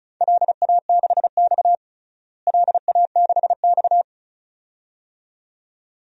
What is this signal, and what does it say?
Morse code, heard as a keyed tone, about 34 wpm, 700 Hz. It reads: LA6X LA6X
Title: DXCC-100 Call Signs - 4 Characters - Repeated 2x - 35wpm